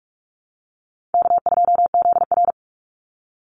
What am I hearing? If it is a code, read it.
Morse code, heard as a keyed tone, about 44 wpm, 700 Hz. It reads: K2ZR
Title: Call Signs Repeated 2x - 45wpm